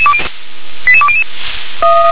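Five-tone selective call (ZVEI), heard as quick runs of five tones, tones of about 70 ms. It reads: 88200
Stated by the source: ZVEI five-tone selective call 88842 received by SDR.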